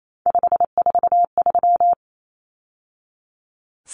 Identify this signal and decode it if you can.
Morse code, heard as a keyed tone, about 28 wpm, 700 Hz. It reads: 543
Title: Numbers - 3-Digit 28wpm